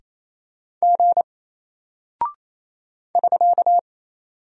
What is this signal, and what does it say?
Morse code, heard as a keyed tone, about 28 wpm, 700 Hz. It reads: G <SK>
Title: Single Character (Letter:Number:Punctuation:Prosign) Alt 28wpm